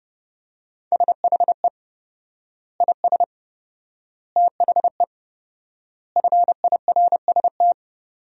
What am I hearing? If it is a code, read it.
Morse code, heard as a keyed tone, about 30 wpm, 700 Hz. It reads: SHE IS THE FIRST
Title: 4x Word-Spacing - Sentences from Top 400 Words - 30wpm